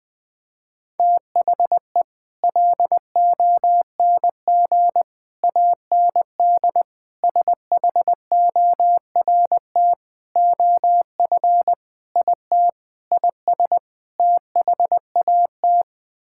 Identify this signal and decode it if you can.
Morse code, heard as a keyed tone, about 20 wpm, 700 Hz. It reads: THE LONG AND SHORT OF IT IS THAT
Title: Sets of 8 Words - 20wpm